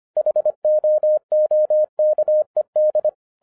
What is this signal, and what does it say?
Morse code, heard as a keyed tone, about 25 wpm, 610 Hz. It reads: HOOKED
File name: FIRST FISTS CONTACT-25wpm